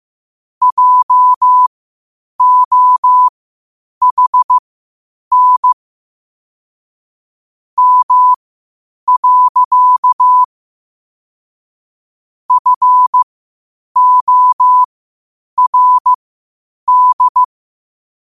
Morse code, heard as a keyed tone, about 15 wpm, spaced as for 8 wpm, 1000 Hz. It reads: JOHN M. FORD
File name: qotdfiles / 8wpm